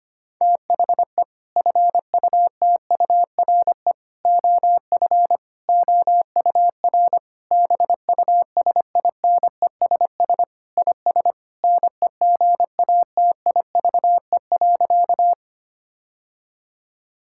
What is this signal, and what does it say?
Morse code, heard as a keyed tone, about 25 wpm, 700 Hz. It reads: THE FUTURE OF OUR BUSINESS IS NEGATIVE.
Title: Sentences from Top 1000 Words - Review Part 6 - 25wpm